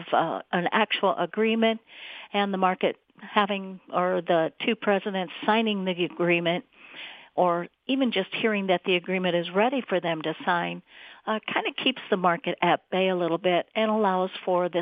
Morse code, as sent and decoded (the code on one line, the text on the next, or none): none